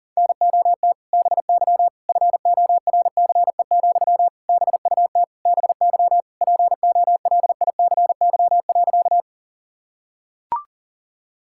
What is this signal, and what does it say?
Morse code, heard as a keyed tone, about 37 wpm, 700 Hz. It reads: NOT BY FORCE, BUT BY POLICY.
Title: Sentences from Top 1000 Words - Review Part 5 - 40wpm